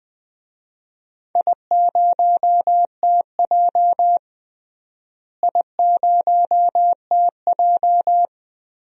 Morse code, heard as a keyed tone, about 20 wpm, 700 Hz. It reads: I0TJ I0TJ
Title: DXCC-100 Call Signs - 4 Characters - Repeated 3x - 20wpm